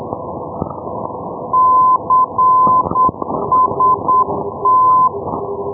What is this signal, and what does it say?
Morse code, heard as a keyed tone, about 8 wpm, 1020 Hz, with noise. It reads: CST
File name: europe / e-297 ndb palma-costix cst-351+f-2078 ndb perpignan pl-351-1305-day-menorca-rb